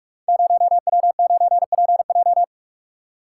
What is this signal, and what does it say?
Morse code, heard as a keyed tone, about 44 wpm, 700 Hz. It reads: 0W9PJ
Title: Code Groups - 5 Letter:Numbers - 45wpm